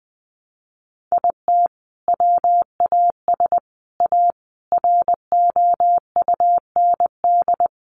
Morse code, heard as a keyed tone, about 20 wpm, 700 Hz. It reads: IT WAS A ROUND
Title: Sentences from Top 400 Words 20wpm